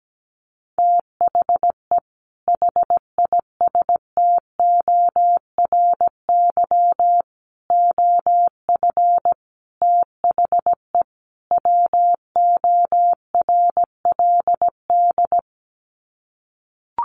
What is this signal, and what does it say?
Morse code, heard as a keyed tone, about 17 wpm, 700 Hz. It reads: THE HISTORY OF THE WORLD
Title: Sets of 5 Words Encore2 17wpm